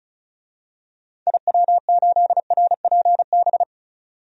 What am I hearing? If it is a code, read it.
Morse code, heard as a keyed tone, about 36 wpm, 700 Hz. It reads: IW8RPB
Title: DXCC-100 Call Signs - 6 Characters - 35wpm